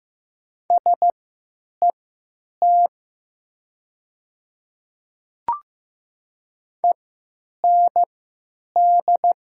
Morse code, heard as a keyed tone, about 15 wpm, spaced as for 9 wpm, 700 Hz. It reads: SET END
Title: 3x Character Spacing - Top 300 Words - 15wpm